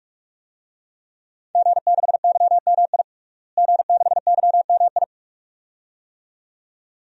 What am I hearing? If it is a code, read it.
Morse code, heard as a keyed tone, about 44 wpm, 700 Hz. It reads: G6YMI G6YMI
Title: DXCC-100 Call Signs - 5 Characters - Repeated 2x - 45wpm